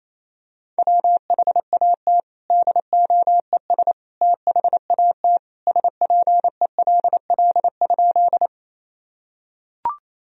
Morse code, heard as a keyed tone, about 28 wpm, 700 Hz. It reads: WHAT DOES THAT SPELL?
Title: Sentences from Top 1000 Words - Review Part 1 - 28wpm